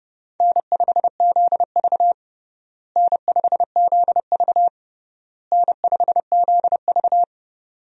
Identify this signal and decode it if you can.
Morse code, heard as a keyed tone, about 30 wpm, 700 Hz. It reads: N5ZV N5ZV N5ZV
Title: USA Call Signs - 4 Characters - Repeated 3x - 30wpm